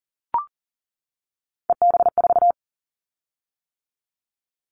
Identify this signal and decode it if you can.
Morse code, heard as a keyed tone, about 40 wpm, 700 Hz. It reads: EB4
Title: Callsign Prefixes for Top 25 DXCC - 40wpm